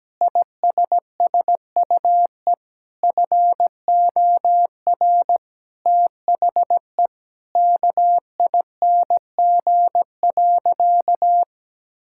Morse code, heard as a keyed tone, about 17 wpm, 700 Hz. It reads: ISSUE FOR THE KING.
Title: Sentences from Top 1000 Words - Review Part 6 - 17wpm